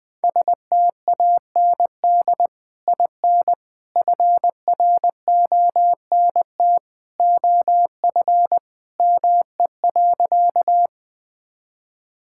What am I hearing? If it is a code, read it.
Morse code, heard as a keyed tone, about 20 wpm, 700 Hz. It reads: STAND IN FRONT OF ME.
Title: Sentences with 2-Word Phrases - 20wpm